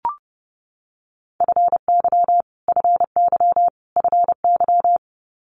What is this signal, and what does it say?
Morse code, heard as a keyed tone, about 30 wpm, 700 Hz. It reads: FY FY FY
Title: Multi-letter Phonemes Repeated 3x 30wpm